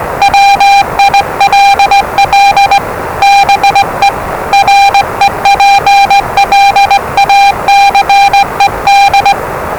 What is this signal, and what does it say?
Morse code, heard as a keyed tone, about 20 wpm, 800 Hz, with noise. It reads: WILL BE REPLACED
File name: WellingtonRadioZlwCloseDownOnHfAndMf / 01-Zlw 4277khz 417.5 qrt until 0925z